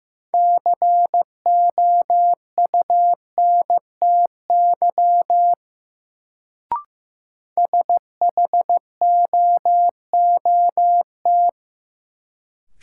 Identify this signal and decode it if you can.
Morse code, heard as a keyed tone, about 15 wpm, 700 Hz. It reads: COUNTY SHOOT
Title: Sentences from Top 1200 Words - New Word Drill - 15wpm